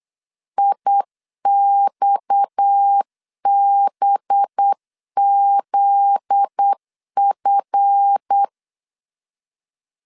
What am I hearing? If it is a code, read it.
Morse code, heard as a keyed tone, about 8 wpm, 800 Hz. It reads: IXBZF
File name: LEZIONI / 9 2 8wpm 3437